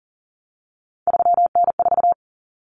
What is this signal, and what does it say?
Morse code, heard as a keyed tone, about 40 wpm, 700 Hz. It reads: 3N4
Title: Common Cut Numbers - 3-Digit 40wpm